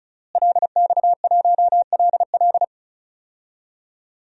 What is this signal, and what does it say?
Morse code, heard as a keyed tone, about 34 wpm, 700 Hz. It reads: LX1LL
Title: DXCC-100 Call Signs - 5 Characters - Repeated 2x - 35wpm